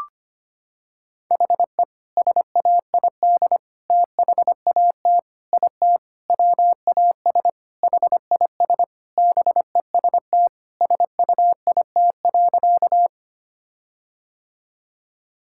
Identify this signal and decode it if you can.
Morse code, heard as a keyed tone, about 25 wpm, 700 Hz. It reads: HE SAID THAT IT WAS HIS BEST SUIT.
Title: Sentences from Top 1600 Words 25wpm